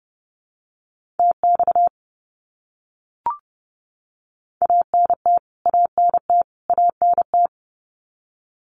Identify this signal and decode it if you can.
Morse code, heard as a keyed tone, about 30 wpm, 700 Hz. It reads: TX ANT ANT ANT
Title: Abbreviations Repeated 3x - 30wpm